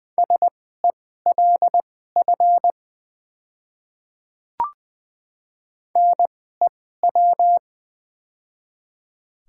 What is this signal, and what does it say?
Morse code, heard as a keyed tone, about 20 wpm, spaced as for 15 wpm, 700 Hz. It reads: SELF NEW
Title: 2x Character Spacing - Top 300 Words - 20wpm